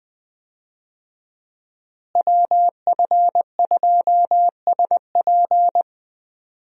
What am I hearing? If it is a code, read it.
Morse code, heard as a keyed tone, about 20 wpm, 700 Hz. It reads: WF2SP
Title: USA Call Signs - 5 Characters - Repeated 3x - 20wpm